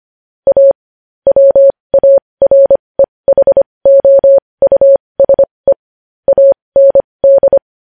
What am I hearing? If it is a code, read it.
Morse code, heard as a keyed tone, about 25 wpm, spaced as for 20 wpm, 550 Hz. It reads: A WAREHOUSE AND
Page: News Headlines in Morse Code at 20 WPM